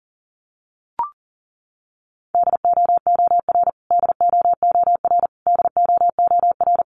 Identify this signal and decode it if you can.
Morse code, heard as a keyed tone, about 40 wpm, 700 Hz. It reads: DOOR DOOR DOOR
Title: Top 300 Words Repeated 40wpm